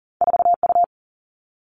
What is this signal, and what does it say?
Morse code, heard as a keyed tone, about 42 wpm, 700 Hz. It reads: 4U
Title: All Cut Numbers - 2-Digit 40wpm